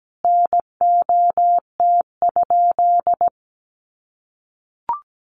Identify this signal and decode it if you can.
Morse code, heard as a keyed tone, about 17 wpm, 700 Hz. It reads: NOT?
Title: Sentences from Top 1300 Words 17wpm